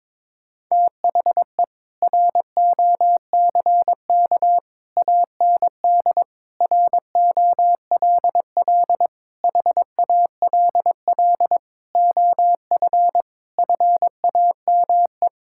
Morse code, heard as a keyed tone, about 22 wpm, 700 Hz. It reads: THE ROCK AND ROLL HALL OF FAME
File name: Sentences with 7-Word Phrases - 22wpm